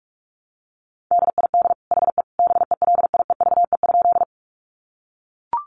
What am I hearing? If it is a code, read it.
Morse code, heard as a keyed tone, about 44 wpm, 700 Hz. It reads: DID HE BELIEVE?
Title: Sentences from Top 1000 Words 45wpm